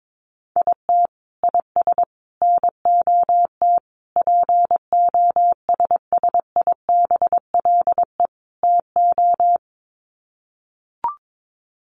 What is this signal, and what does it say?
Morse code, heard as a keyed tone, about 22 wpm, 700 Hz. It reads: IT IS NOT POSSIBLE TO
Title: Sets of 5 Words Encore2 22wpm